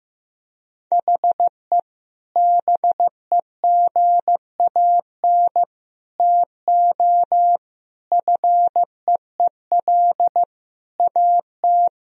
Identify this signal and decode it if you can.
Morse code, heard as a keyed tone, about 15 wpm, 700 Hz. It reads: HE BEGAN TO FEEL AT
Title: Sentences from Top 1000 Words - Review Part 1 - 15wpm